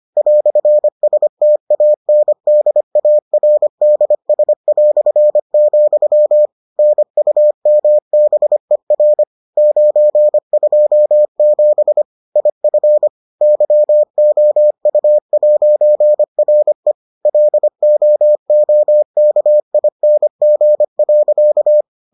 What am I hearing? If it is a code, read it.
Morse code, heard as a keyed tone, about 25 wpm, 600 Hz. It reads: "STANDARDS", NUMBER 927 IF YOU'RE LOOKING.